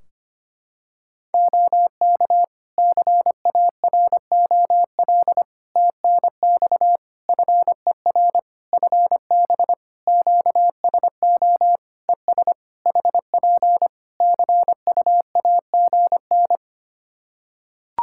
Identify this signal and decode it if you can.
Morse code, heard as a keyed tone, about 25 wpm, 700 Hz. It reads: OK CAROL TNX FER FB QSO ES HP CUAGN
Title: Practice QSO - Exchange 4 - Ending - 25wpm